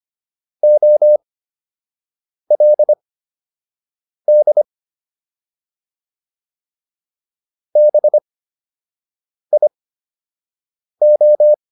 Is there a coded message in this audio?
Yes: Morse code from 0.6 to 11.5 s, heard as a keyed tone, about 25 wpm, spaced as for 6 wpm, 600 Hz. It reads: OLD BIO